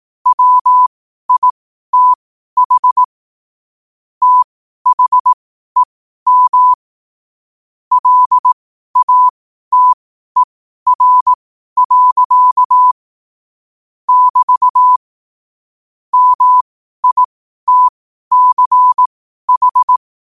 Morse code, heard as a keyed tone, about 18 wpm, spaced as for 12 wpm, 1000 Hz. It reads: WITH THEM LATER. <BT> MITCH
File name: qotdfiles / 12wpm